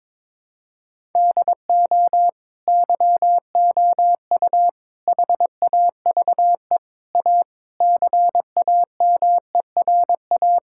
Morse code, heard as a keyed tone, about 21 wpm, 700 Hz. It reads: DO YOU HAVE A CAMERA
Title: Sentences from Top 1000 Words - Review Part 5 - 22wpm